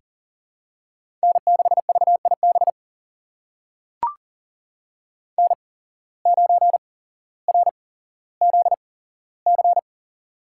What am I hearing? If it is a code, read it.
Morse code, heard as a keyed tone, about 41 wpm, spaced as for 17 wpm, 700 Hz. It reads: N6VIB N9RZC
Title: USA Call Signs - 8x Character Spacing - 40wpm